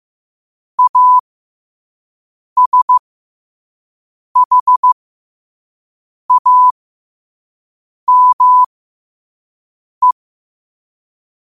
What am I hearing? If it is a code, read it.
Morse code, heard as a keyed tone, about 15 wpm, spaced as for 5 wpm, 1000 Hz. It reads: ASHAME